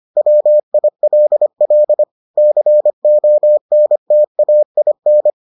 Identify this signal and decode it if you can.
Morse code, heard as a keyed tone, about 25 wpm, 600 Hz. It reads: WILL CONTAIN